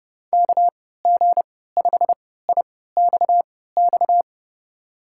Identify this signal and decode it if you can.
Morse code, heard as a keyed tone, about 32 wpm, spaced as for 17 wpm, 700 Hz. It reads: KG5IXX
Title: USA Call Signs - 6 Characters - 3x Character Spacing - 30wpm